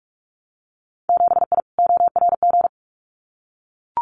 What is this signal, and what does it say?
Morse code, heard as a keyed tone, about 44 wpm, 700 Hz. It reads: 7I ORG
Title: Winter Field Day Exchanges Encore 45wpm